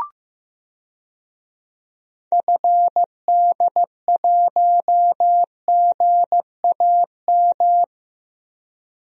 Morse code, heard as a keyed tone, about 15 wpm, 700 Hz. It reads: FD1GAM
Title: DXCC-100 Call Signs - 6 Characters - Repeated 2x - 15wpm